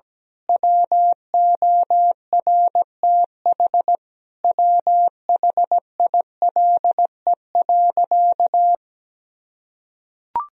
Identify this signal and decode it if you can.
Morse code, heard as a keyed tone, about 17 wpm, 700 Hz. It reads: WORTH WHILE.
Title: Sentences from Top 1000 Words - Review Part 1 - 17wpm